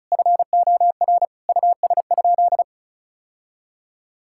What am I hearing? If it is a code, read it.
Morse code, heard as a keyed tone, about 34 wpm, 690 Hz. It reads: FOR US?
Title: Sentences from Top 1000 Words - Review Part 5 - 35wpm